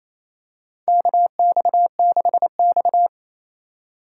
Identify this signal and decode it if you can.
Morse code, heard as a keyed tone, about 28 wpm, 700 Hz. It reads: KX6X
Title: K1USN SST QSO Practice - 28wpm